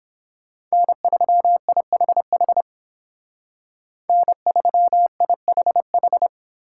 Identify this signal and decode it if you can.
Morse code, heard as a keyed tone, about 29 wpm, 700 Hz. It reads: N3IHH N3IHH